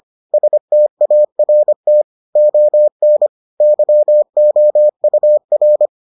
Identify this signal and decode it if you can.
Morse code, heard as a keyed tone, about 25 wpm, 600 Hz. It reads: START ON YOUR